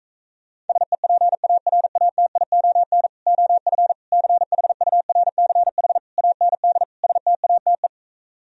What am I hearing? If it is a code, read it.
Morse code, heard as a keyed tone, about 38 wpm, 700 Hz. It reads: SEPARATION OF CHURCH AND STATE